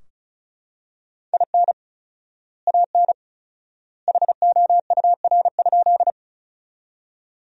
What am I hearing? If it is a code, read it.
Morse code, heard as a keyed tone, about 36 wpm, 700 Hz. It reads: IN AN HOUR?
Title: 4x Word-Spacing - Sentences from Top 400 Words - 35wpm